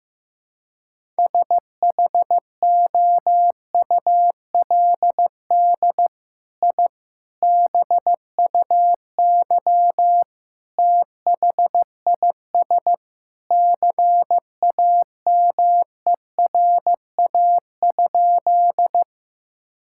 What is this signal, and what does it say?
Morse code, heard as a keyed tone, about 15 wpm, 700 Hz. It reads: SHOULD I BUY THIS CAMERA?
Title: Sentences from Top 1000 Words - Review Part 5 - 15wpm